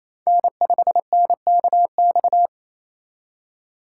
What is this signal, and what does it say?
Morse code, heard as a keyed tone, about 36 wpm, 700 Hz. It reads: N5NKX